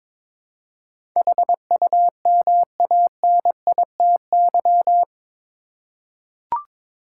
Morse code, heard as a keyed tone, about 22 wpm, 700 Hz. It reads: HUMANITY
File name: practice / 22wpm / Eight Letter Words - 22wpm